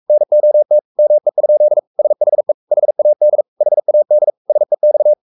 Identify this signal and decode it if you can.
Morse code, heard as a keyed tone, about 44 wpm, 600 Hz. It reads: NOT ME? SHE HAD HAD SEX
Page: Romanen Never af Ken Follet på engelsk.